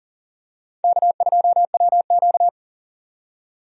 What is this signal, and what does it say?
Morse code, heard as a keyed tone, about 40 wpm, 700 Hz. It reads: K2WQ
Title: USA Call Signs - 4 Characters - 40wpm